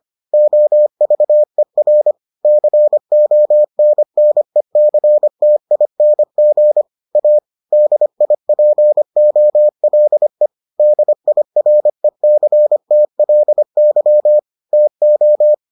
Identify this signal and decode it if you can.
Morse code, heard as a keyed tone, about 25 wpm, 600 Hz. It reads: OVER CONNECTING A DIPOLE DIRECTLY TO